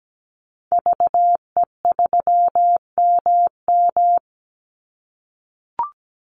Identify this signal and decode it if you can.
Morse code, heard as a keyed tone, about 17 wpm, 700 Hz. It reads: VE3MM
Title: Call Signs - 17wpm